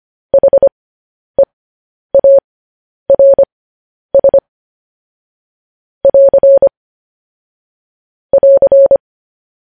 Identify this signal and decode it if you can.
Morse code, heard as a keyed tone, about 25 wpm, spaced as for 10 wpm, 550 Hz. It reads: HEARS <AR> <AR>
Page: news headlines in morse code at 10 wpm